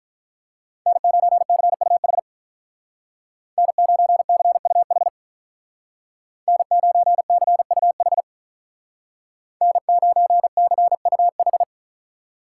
Morse code, heard as a keyed tone, about 43 wpm, 700 Hz. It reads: N9CUH N9CUH N9CUH N9CUH
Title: Speed Racing - USA 1x3 Call Signs 35wpm